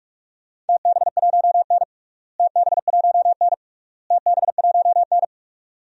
Morse code, heard as a keyed tone, about 45 wpm, 700 Hz. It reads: TB1N TB1N TB1N
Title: DXCC-100 Call Signs - 4 Characters - Repeated 3x - 45wpm